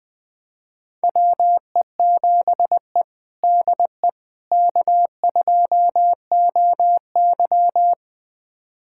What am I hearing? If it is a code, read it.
Morse code, heard as a keyed tone, about 20 wpm, 700 Hz. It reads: WE7E DE K2OY
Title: USA Callsign DE Callsign - 4 Characters - 20wpm